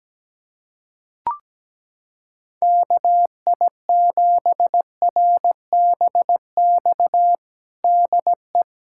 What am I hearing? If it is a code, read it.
Morse code, heard as a keyed tone, about 17 wpm, 700 Hz. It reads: KI7RBX DE
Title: USA Callsign DE Callsign - 6 Characters - 17wpm